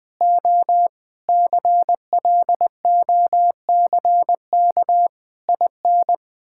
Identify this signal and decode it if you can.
Morse code, heard as a keyed tone, about 20 wpm, 700 Hz. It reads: O CLOCK IN
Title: Pangram Challenge v2 - 20wpm